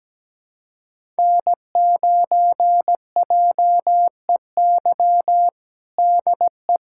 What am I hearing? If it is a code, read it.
Morse code, heard as a keyed tone, about 17 wpm, 700 Hz. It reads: N9JEY DE
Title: USA Callsign DE Callsign - 5 Characters - 17wpm